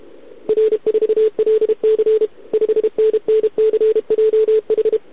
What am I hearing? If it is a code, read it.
Morse code, heard as a keyed tone, about 32 wpm, 440 Hz. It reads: R4LC 5NNCJH
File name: ARCHIV / OKDX2020 / OK1RIb